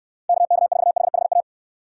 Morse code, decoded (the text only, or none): KK4HHA